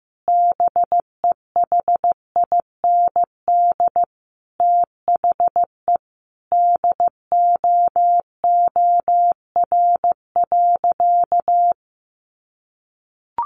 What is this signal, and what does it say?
Morse code, heard as a keyed tone, about 15 wpm, 700 Hz. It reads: BEHIND THE DOOR.
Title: Sentences from Top 500 Words 15wpm